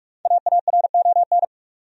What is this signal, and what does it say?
Morse code, heard as a keyed tone, about 46 wpm, 700 Hz. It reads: AARON
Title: USA Common Names - Max 5 Letters - 45wpm